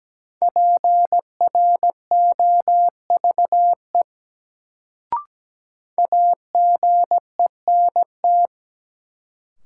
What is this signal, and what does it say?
Morse code, heard as a keyed tone, about 17 wpm, 700 Hz. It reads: PROVE AGENT
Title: Sentences from Top 1600 Words - New Word Drill Alt - 17wpm